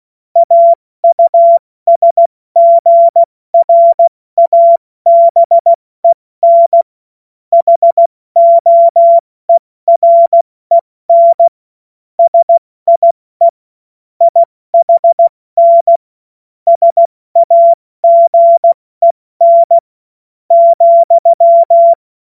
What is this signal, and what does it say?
Morse code, heard as a keyed tone, about 16 wpm, spaced as for 14 wpm, 680 Hz. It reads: AUSGRABEN HOEREN SIE IHN SAGEN,
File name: Morsekurs / Witze / Witz 07. Dezember 2021, 70 F80